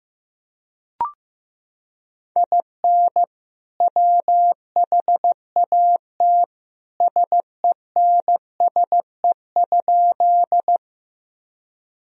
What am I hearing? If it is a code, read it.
Morse code, heard as a keyed tone, about 15 wpm, 700 Hz. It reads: IN WHAT SENSE?
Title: Sentences from Top 1000 Words - Review Part 4 - 15wpm